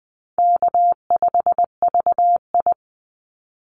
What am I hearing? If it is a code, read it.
Morse code, heard as a keyed tone, about 20 wpm, 700 Hz. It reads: K5VI
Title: Easy Call Signs Repeated 2x - 20wpm